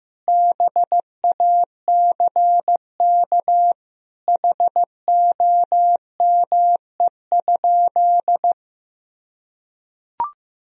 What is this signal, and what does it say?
Morse code, heard as a keyed tone, about 15 wpm, 700 Hz. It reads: BACK HOME?
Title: Sentences from Top 1000 Words - Review Part 2 - 15wpm